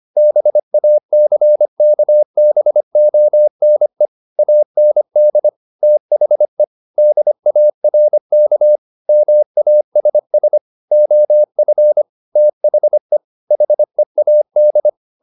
Morse code, heard as a keyed tone, about 25 wpm, 600 Hz. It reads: BACKBONE AND THE DARK MASS OF THE HEAD